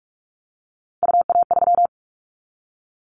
Morse code, heard as a keyed tone, about 46 wpm, 690 Hz. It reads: UA3
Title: Callsign Prefixes for Top 50 DXCC - 45wpm